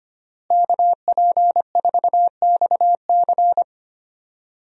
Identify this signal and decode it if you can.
Morse code, heard as a keyed tone, about 25 wpm, 700 Hz. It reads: KP4XC
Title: DXCC-100 Call Signs - 5 Characters - Repeated 3x - 25wpm